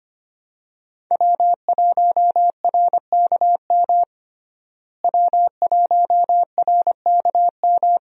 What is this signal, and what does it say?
Morse code, heard as a keyed tone, about 25 wpm, 700 Hz. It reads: W1RKM W1RKM
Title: USA Call Signs - 5 Characters - Repeated 2x - 25wpm